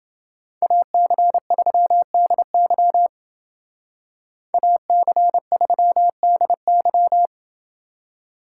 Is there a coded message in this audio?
Yes: Morse code from 0.6 to 7.3 s, heard as a keyed tone, about 28 wpm, 700 Hz. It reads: AC3DY AC3DY